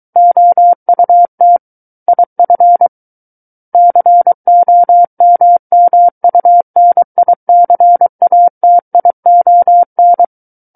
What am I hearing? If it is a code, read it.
Morse code, heard as a keyed tone, about 23 wpm, 700 Hz. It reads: OUT IF COMMUNICATION
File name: Book One / 14-Chapter 14 - 23 WPM